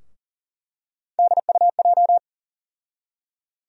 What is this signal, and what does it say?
Morse code, heard as a keyed tone, about 40 wpm, 700 Hz. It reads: DUJ
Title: USA Call Sign Postfixes - 3 Letters 40wpm